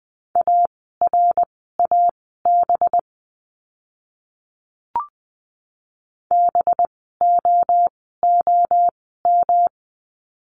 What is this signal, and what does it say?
Morse code, heard as a keyed tone, about 20 wpm, spaced as for 15 wpm, 700 Hz. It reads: ARAB BOOM
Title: 185 Lesson 30f - Letter b - Words 20wpm